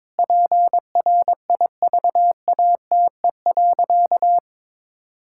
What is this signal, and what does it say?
Morse code, heard as a keyed tone, about 22 wpm, 700 Hz. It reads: PRIVATE.
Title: Sentences from Top 1600 Words 22wpm